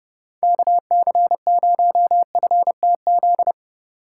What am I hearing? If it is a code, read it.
Morse code, heard as a keyed tone, about 30 wpm, 700 Hz. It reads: KC0FTZ
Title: USA Call Signs - 6x Character Spacing - 30wpm